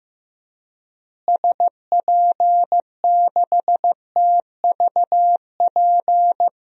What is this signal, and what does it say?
Morse code, heard as a keyed tone, about 15 wpm, 700 Hz. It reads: SP6TVP
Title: DXCC-100 Call Signs - 6 Characters - Repeated 2x - 15wpm